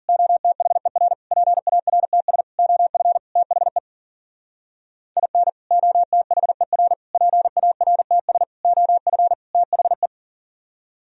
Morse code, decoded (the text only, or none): OTHER PARTS OF THE IN OTHER PARTS OF THE